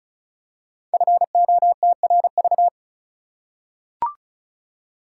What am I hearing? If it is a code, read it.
Morse code, heard as a keyed tone, about 36 wpm, 700 Hz. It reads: FOTRV
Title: Code Groups - 5 Letters - 35wpm